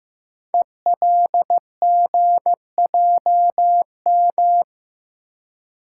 Morse code, heard as a keyed tone, about 15 wpm, 700 Hz. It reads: ELGJM
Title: Code Groups - 5 Letters - 15wpm